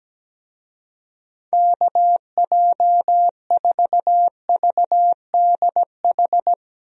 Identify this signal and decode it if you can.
Morse code, heard as a keyed tone, about 17 wpm, 700 Hz. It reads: KJ4VDH
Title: USA Call Signs - 6 Characters - Repeated 2x - 17wpm